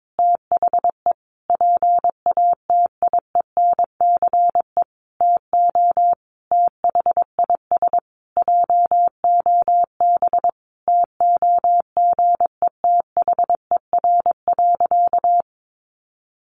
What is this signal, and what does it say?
Morse code, heard as a keyed tone, about 22 wpm, 700 Hz. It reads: THE PATIENCE TO THIS JOB TOGETHER.
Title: Sentences from Top 1300 Words 22wpm